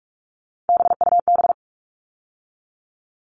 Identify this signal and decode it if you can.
Morse code, heard as a keyed tone, about 44 wpm, 700 Hz. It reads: BUB
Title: All Cut Numbers - 3-Digit 45wpm